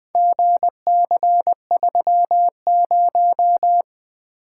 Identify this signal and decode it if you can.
Morse code, heard as a keyed tone, about 20 wpm, 700 Hz. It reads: GC30
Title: Four Letter-Number Combinations - 20wpm